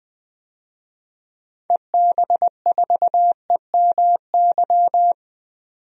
Morse code, heard as a keyed tone, about 20 wpm, 700 Hz. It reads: EB4EMY
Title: DXCC-100 Call Signs - 6 Characters - 20wpm